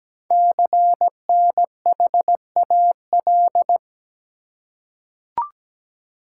Morse code, heard as a keyed tone, about 17 wpm, 700 Hz. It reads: CNHAL
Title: Code Groups - 5 Letters - 17wpm